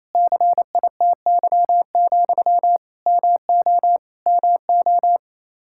Morse code, heard as a keyed tone, about 28 wpm, 700 Hz. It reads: CITY, MO MO
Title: Practice QSO - Exchange 1 - Beginning - 28wpm